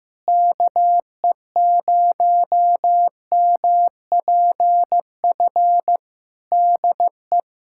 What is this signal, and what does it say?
Morse code, heard as a keyed tone, about 15 wpm, 700 Hz. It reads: KE0MPF DE
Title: Practice QSO - Exchange 4 - Ending - 15wpm